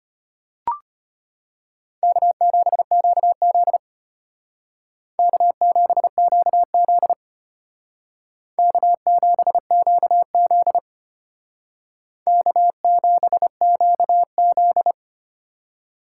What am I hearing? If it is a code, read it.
Morse code, heard as a keyed tone, about 32 wpm, 700 Hz. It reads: K7QZ K7QZ K7QZ K7QZ